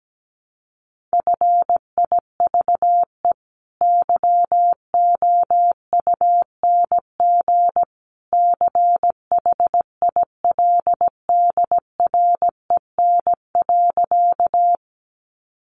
Morse code, heard as a keyed tone, about 17 wpm, 700 Hz. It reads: FIVE YOUNG CHILDREN.
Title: Sentences from Top 400 Words 17wpm